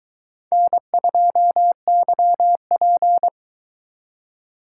Morse code, heard as a keyed tone, about 22 wpm, 700 Hz. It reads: N2YP